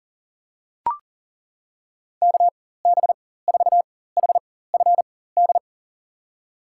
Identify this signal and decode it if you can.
Morse code, heard as a keyed tone, about 37 wpm, spaced as for 18 wpm, 700 Hz. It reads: KB4HFD
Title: USA Call Signs - 6 Characters - 4x Character Spacing - 40wpm